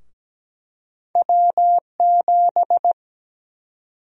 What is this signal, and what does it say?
Morse code, heard as a keyed tone, about 17 wpm, 700 Hz. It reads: W7
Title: USA Call Sign Prefixes - Repeated 2x - 17wpm